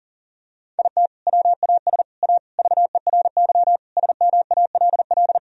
Morse code, heard as a keyed tone, about 40 wpm, 700 Hz. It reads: IT WAS A VERY SMALL
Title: Sentences from Top 1000 Words - Review Part 1 - 40wpm